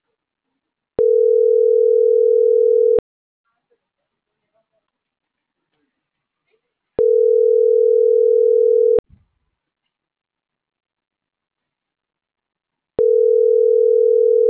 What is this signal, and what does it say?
Ringback tone (North America), 440 + 480 Hz.
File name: CENTRAL MAERSK / out gabrielle.melendez - .48138